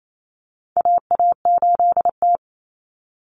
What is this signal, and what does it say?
Morse code, heard as a keyed tone, about 28 wpm, 700 Hz. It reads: AA8T
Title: Common Cut Numbers - 4-Digit 28wpm